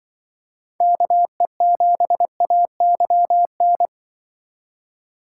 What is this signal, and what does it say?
Morse code, heard as a keyed tone, about 24 wpm, 700 Hz. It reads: KE7AYN
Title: Speed Racing - USA 2x3 Call Signs 20wpm